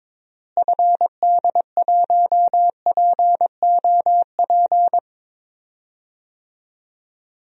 Morse code, heard as a keyed tone, about 22 wpm, 700 Hz. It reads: FD1POP
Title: DXCC-100 Call Signs - 6 Characters - Repeated 2x - 22wpm